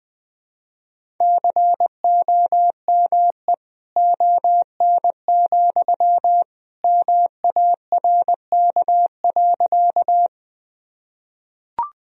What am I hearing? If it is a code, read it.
Morse code, heard as a keyed tone, about 21 wpm, 700 Hz. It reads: COME ON, MARK.
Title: Sentences from Top 1000 Words - Review Part 1 - 20wpm